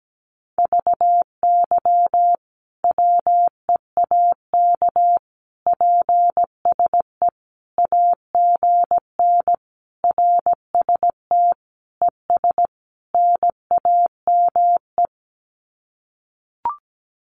Morse code, heard as a keyed tone, about 17 wpm, 700 Hz. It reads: VY WEAK PSE AGN RST ES NAME
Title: Common QSO Phrases 17wpm